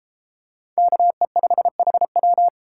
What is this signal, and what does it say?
Morse code, heard as a keyed tone, about 32 wpm, 700 Hz. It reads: KE5HW